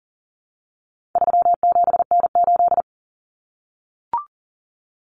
Morse code, heard as a keyed tone, about 40 wpm, 700 Hz. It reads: 37N8
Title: Common Cut Numbers - 4-Digit 40wpm